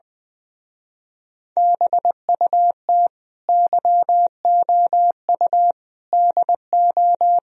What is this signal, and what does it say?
Morse code, heard as a keyed tone, about 22 wpm, 700 Hz. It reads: BUT YOU DO
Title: Speed-Racing - Sets of 4 Words 20wpm